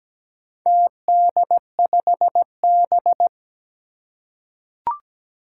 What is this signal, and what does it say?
Morse code, heard as a keyed tone, about 17 wpm, 700 Hz. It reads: TD5B
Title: Four Challenging Letter-Number Combinations 17wpm